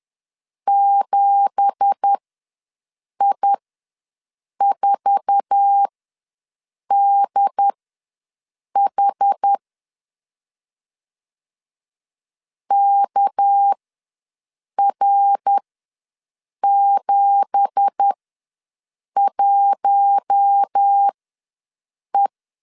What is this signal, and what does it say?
Morse code, heard as a keyed tone, about 10 wpm, spaced as for 6 wpm, 800 Hz. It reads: 7I4DH KR71E
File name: LEZIONI / 9 2867 ws 10wpm 4215